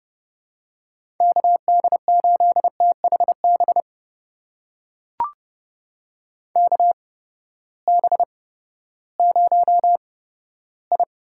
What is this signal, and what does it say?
Morse code, heard as a keyed tone, about 29 wpm, spaced as for 15 wpm, 700 Hz. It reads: KD8THB KB0I
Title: USA Call Signs - 6 Characters - 8x Character Spacing - 30wpm